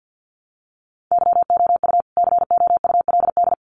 Morse code, heard as a keyed tone, about 48 wpm, 700 Hz. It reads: YOU COULD
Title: Sets of 2 Words 50wpm